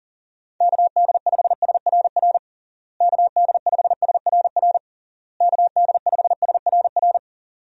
Morse code, heard as a keyed tone, about 40 wpm, 700 Hz. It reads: KD5SRR KD5SRR KD5SRR
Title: USA Call Signs - 6 Characters - Repeated 3x - 40wpm